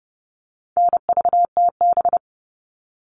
Morse code, heard as a keyed tone, about 30 wpm, 700 Hz. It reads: NVTB
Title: All Cut Numbers - 4-Digit 30wpm